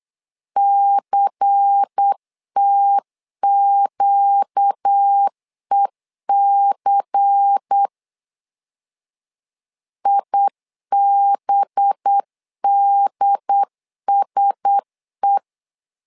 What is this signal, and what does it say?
Morse code, heard as a keyed tone, about 8 wpm, 800 Hz. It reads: CTQEC IBDSE
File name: LEZIONI / 5 LBVC / 5 LBVC 8wpm 2935